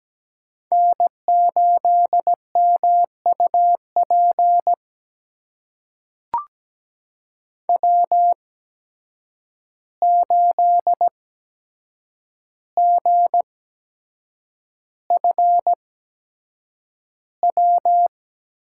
Morse code, heard as a keyed tone, about 17 wpm, spaced as for 7 wpm, 700 Hz. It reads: N8MUP W8GFW
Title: USA Call Signs - 5 Characters - 8x Character Spacing - 17wpm